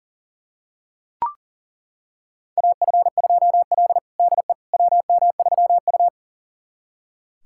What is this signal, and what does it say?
Morse code, heard as a keyed tone, about 37 wpm, 700 Hz. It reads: AF2L DE WM3U
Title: USA Callsign DE Callsign - 4 Characters - 40wpm